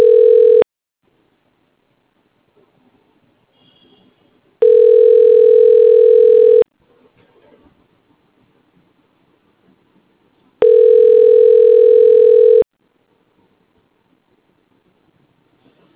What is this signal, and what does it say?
Ringback tone (North America), 440 + 480 Hz.